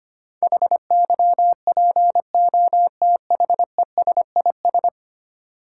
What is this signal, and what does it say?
Morse code, heard as a keyed tone, about 25 wpm, 700 Hz. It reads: HYPOTHESIS
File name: 10 Letter Words - First 3 Letters Given - 25wpm